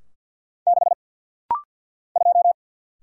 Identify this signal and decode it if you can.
Morse code, heard as a keyed tone, about 50 wpm, 700 Hz. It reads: 6 2
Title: Single Letter-Number - Rapid-Fire 50wpm